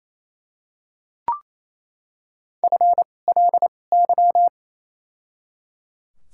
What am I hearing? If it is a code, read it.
Morse code, heard as a keyed tone, about 28 wpm, 700 Hz. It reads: F L Y
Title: 2x Character Spacing - Top 500 Words - 28wpm